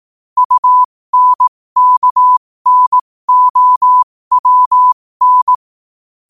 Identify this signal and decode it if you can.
Morse code, heard as a keyed tone, about 18 wpm, spaced as for 15 wpm, 1000 Hz. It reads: UNKNOWN
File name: qotdfiles / 15wpm